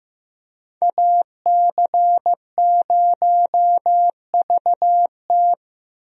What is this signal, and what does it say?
Morse code, heard as a keyed tone, about 15 wpm, 700 Hz. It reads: AC0VT
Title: USA Call Signs - 5 Characters - Repeated 2x - 15wpm